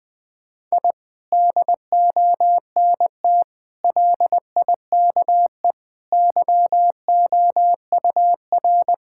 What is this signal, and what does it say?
Morse code, heard as a keyed tone, about 20 wpm, 700 Hz. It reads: I DONT LIKE YOUR
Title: Sentences from Top 1000 Words - Review Part 6 - 20wpm